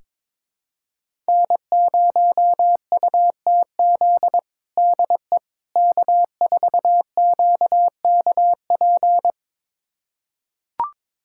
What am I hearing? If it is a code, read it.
Morse code, heard as a keyed tone, about 22 wpm, 700 Hz. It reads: N0UTZ DE K4QKP
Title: USA Callsign DE Callsign - 5 Characters - 22wpm